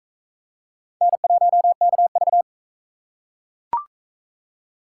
Morse code, heard as a keyed tone, about 41 wpm, 700 Hz. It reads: N1KV